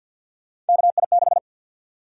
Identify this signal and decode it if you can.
Morse code, heard as a keyed tone, about 52 wpm, 700 Hz. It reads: KI6
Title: USA Call Sign Prefixes - 50wpm